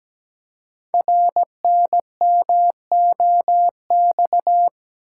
Morse code, heard as a keyed tone, about 17 wpm, 700 Hz. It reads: RNMOX
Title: Code Groups - 5 Letters - 17wpm